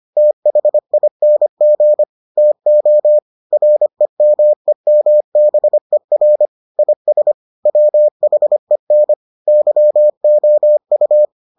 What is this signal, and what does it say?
Morse code, heard as a keyed tone, about 25 wpm, 600 Hz. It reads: THING TO REMEMBER IS WHEN YOU